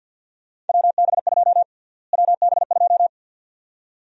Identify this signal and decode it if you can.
Morse code, heard as a keyed tone, about 51 wpm, 700 Hz. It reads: WB2 WB2
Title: USA Call Sign Prefixes - Repeated 2x - 50wpm